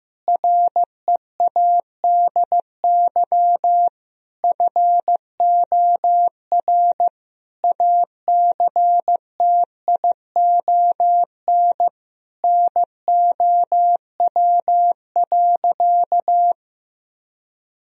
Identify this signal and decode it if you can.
Morse code, heard as a keyed tone, about 15 wpm, 700 Hz. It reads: READY FOR ACTION NOW.
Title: Sentences from Top 1000 Words - Review Part 4 - 15wpm